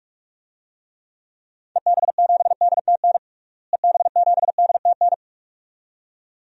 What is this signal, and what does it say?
Morse code, heard as a keyed tone, about 46 wpm, 700 Hz. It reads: EB7DTN EB7DTN
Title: DXCC-100 Call Signs - 6 Characters - Repeated 2x - 45wpm